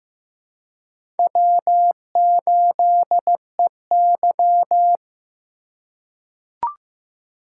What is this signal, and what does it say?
Morse code, heard as a keyed tone, about 15 wpm, 700 Hz. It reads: W8EY
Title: USA Call Signs - 4x Character Spacing - 15wpm